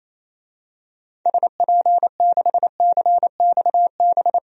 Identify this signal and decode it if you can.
Morse code, heard as a keyed tone, about 28 wpm, 700 Hz. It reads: SP6CXB
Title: DXCC-100 Call Signs - 6 Characters - 28wpm